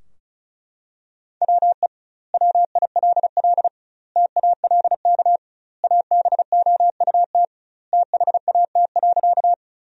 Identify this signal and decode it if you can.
Morse code, heard as a keyed tone, about 36 wpm, 700 Hz. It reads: WE WILL TALK ABOUT THAT.
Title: 2x Word-Spacing - Sentences from Top 400 Words - 35wpm